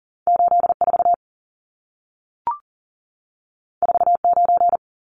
Morse code, heard as a keyed tone, about 40 wpm, 700 Hz. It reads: 84 49
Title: Numbers - 2-Digit 40wpm